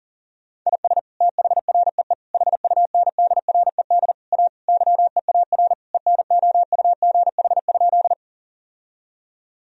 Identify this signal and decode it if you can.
Morse code, heard as a keyed tone, about 40 wpm, 700 Hz. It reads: IS THREE HUNDRED A YEAR ENOUGH?
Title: Sentences from Top 1000 Words - Review Part 2 - 40wpm